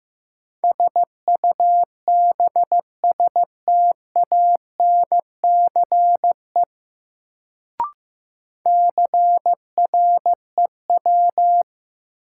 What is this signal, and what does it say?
Morse code, heard as a keyed tone, about 16 wpm, 700 Hz. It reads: SUBSTANCE CREW
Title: Sentences from Top 1500 Words - New Word Drill - 15wpm